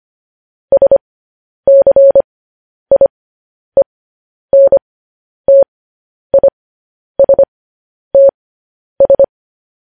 Morse code, heard as a keyed tone, about 25 wpm, spaced as for 10 wpm, 550 Hz. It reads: SCIENTISTS
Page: News Headlines in Morse Code at 10 WPM